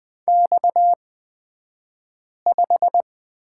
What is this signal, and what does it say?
Morse code, heard as a keyed tone, about 20 wpm, spaced as for 5 wpm, 700 Hz. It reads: X5
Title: 228 Lesson 37d - Letter x - Two Characters 20wpm